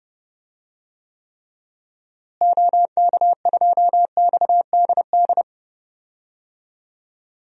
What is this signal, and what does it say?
Morse code, heard as a keyed tone, about 30 wpm, 700 Hz. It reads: OK2XDD
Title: DXCC-100 Call Signs - 6 Characters - Repeated 2x - 30wpm